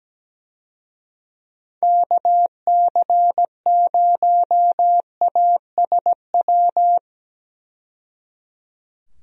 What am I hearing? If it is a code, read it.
Morse code, heard as a keyed tone, about 17 wpm, 700 Hz. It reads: KC0ASW
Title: USA Call Signs - 6 Characters - 17wpm